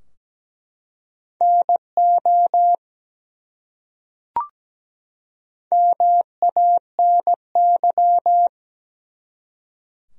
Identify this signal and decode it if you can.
Morse code, heard as a keyed tone, about 17 wpm, 700 Hz. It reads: NO MANY
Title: Top 100 Words 17wpm